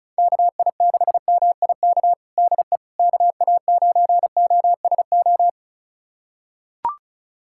Morse code, heard as a keyed tone, about 36 wpm, 700 Hz. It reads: KI6MIK DE KA9OSO
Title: USA Callsign DE Callsign - 6 Characters - 35wpm